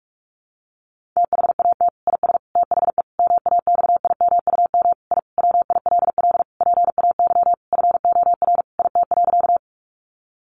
Morse code, heard as a keyed tone, about 44 wpm, 700 Hz. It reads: THAT IS THE MAXIMUM I WILL PAY FOR IT.
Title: Sentences from Top 1300 Words 45wpm